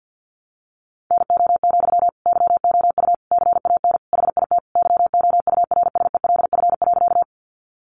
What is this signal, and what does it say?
Morse code, heard as a keyed tone, about 50 wpm, 700 Hz. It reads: NO, YOU CAN HIT YOURSELF.
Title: Sentences from Top 1200 Words 50wpm